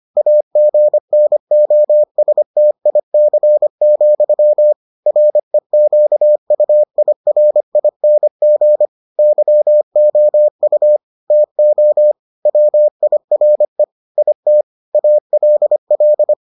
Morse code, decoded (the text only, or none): AGNOSTIC, REQUIRING YOU TO WIRE IT ALL